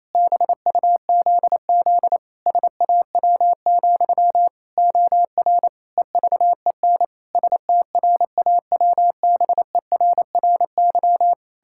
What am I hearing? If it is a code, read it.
Morse code, heard as a keyed tone, about 27 wpm, 700 Hz. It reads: BUZZ SAW, OR EVEN STRAWBERRY
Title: Pangram Challenge v2 - 28wpm